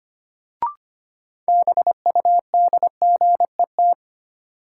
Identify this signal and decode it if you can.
Morse code, heard as a keyed tone, about 25 wpm, 700 Hz. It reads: BUDGET
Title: Sentences from Top 1200 Words - New Word Drill Alt - 25wpm